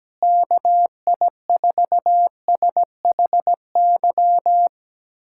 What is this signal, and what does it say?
Morse code, heard as a keyed tone, about 17 wpm, 700 Hz. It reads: KI4SHY
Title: USA Callsign DE Callsign - 6 Characters - 17wpm